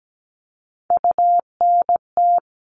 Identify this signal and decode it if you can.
Morse code, heard as a keyed tone, about 17 wpm, 700 Hz. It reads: UNT
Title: All Cut Numbers - 3-Digit 17wpm